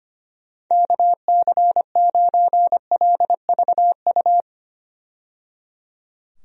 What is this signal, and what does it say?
Morse code, heard as a keyed tone, about 25 wpm, 700 Hz. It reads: KC9LVU
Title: USA Call Signs - 6 Characters - 25wpm